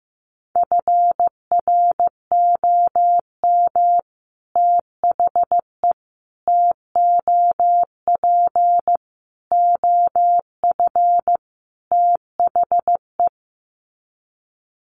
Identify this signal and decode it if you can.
Morse code, heard as a keyed tone, about 15 wpm, 700 Hz. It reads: FROM THE TOP OF THE
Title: Sets of 5 Words Encore 15wpm